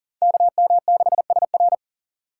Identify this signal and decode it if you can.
Morse code, heard as a keyed tone, about 40 wpm, 700 Hz. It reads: KM6SR
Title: USA Call Signs - 5 Characters - 2x Character Spacing - 40wpm